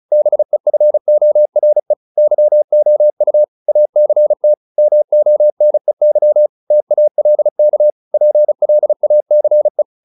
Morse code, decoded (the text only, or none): BEFORE YOU ACT MONEY TALK PLACE